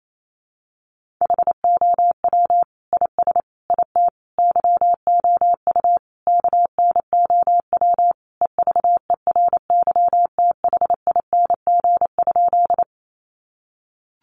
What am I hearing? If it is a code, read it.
Morse code, heard as a keyed tone, about 28 wpm, 700 Hz. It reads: HOW IS IT YOU KNOW EVERYTHING?
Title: Sentences from Top 700 Words 28wpm